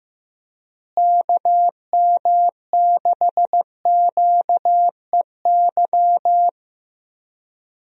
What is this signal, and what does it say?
Morse code, heard as a keyed tone, about 15 wpm, 700 Hz. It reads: KM6QEY
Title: USA Call Signs - 6 Characters - Repeated 2x - 15wpm